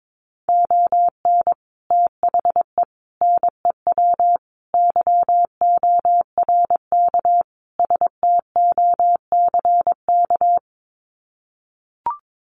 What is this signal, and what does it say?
Morse code, heard as a keyed tone, about 22 wpm, 700 Hz. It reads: ON THE NEW YORK STOCK
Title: Sets of 5 Words Encore 22wpm